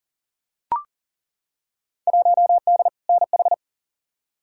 Morse code, heard as a keyed tone, about 40 wpm, 700 Hz. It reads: 1D NH
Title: Field Day Exchanges - 40wpm